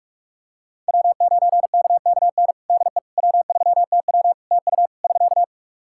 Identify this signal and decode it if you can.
Morse code, heard as a keyed tone, about 44 wpm, 700 Hz. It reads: W9KKN DE W3TW TU <SK>
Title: Practice QSO - Exchange 4 - Ending - 45wpm